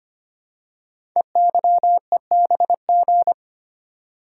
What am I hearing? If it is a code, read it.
Morse code, heard as a keyed tone, about 25 wpm, 700 Hz. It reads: EYEBG
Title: Code Groups - 5 Letters - 25wpm